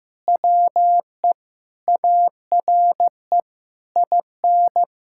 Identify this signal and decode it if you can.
Morse code, heard as a keyed tone, about 15 wpm, 700 Hz. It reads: WE ARE IN
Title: Sentences from Top 1600 Words 15wpm